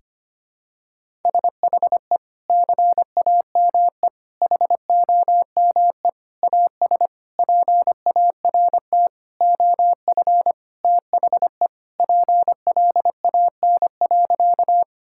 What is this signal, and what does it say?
Morse code, heard as a keyed tone, about 25 wpm, 700 Hz. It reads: SHE CAME HOME AS PART OF THE PLAN.
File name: practice / 25wpm / Sentences with 4-Word Phrases - 25wpm